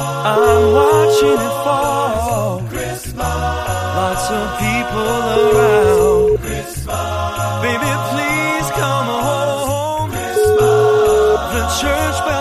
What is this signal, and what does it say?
Ringback tone (Europe, 425 Hz), with noise.